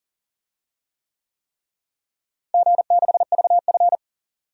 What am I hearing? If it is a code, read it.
Morse code, heard as a keyed tone, about 40 wpm, 700 Hz. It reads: G6VF
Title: DXCC-100 Call Signs - 4 Characters - Repeated 3x - 40wpm